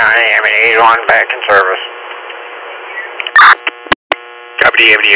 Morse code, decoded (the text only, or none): none